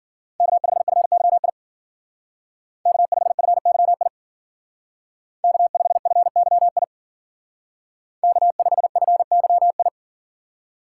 Morse code, heard as a keyed tone, about 52 wpm, 700 Hz. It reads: K5FYI K5FYI K5FYI K5FYI